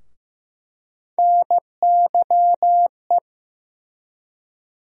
Morse code, heard as a keyed tone, about 15 wpm, 700 Hz. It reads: NYE
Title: USA Call Sign Postfixes - 3 Letters 15wpm